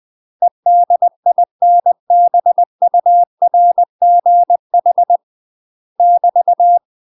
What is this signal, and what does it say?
Morse code, heard as a keyed tone, about 20 wpm, 700 Hz. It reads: EDINBURGH <BT>
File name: UK MASTER QSOs TRAINING / 20WPM / UK MASTER QSOs 0003